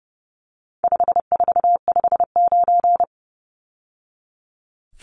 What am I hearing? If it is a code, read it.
Morse code, heard as a keyed tone, about 30 wpm, 700 Hz. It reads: 5459
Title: Numbers - 4-Digit 30wpm